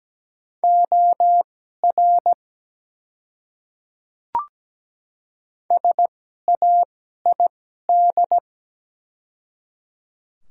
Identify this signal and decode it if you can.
Morse code, heard as a keyed tone, about 17 wpm, spaced as for 12 wpm, 700 Hz. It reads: OR SAID
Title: 2x Character Spacing - Top 300 Words - 17wpm